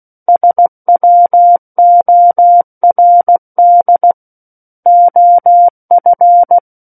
Morse code, heard as a keyed tone, about 16 wpm, 700 Hz. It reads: SWORD OF
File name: Book One / 07-Chapter 7 - 16 WPM